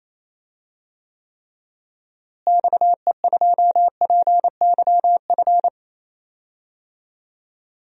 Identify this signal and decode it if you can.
Morse code, heard as a keyed tone, about 28 wpm, 700 Hz. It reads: XE2PYF
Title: DXCC-100 Call Signs - 6 Characters - Repeated 2x - 28wpm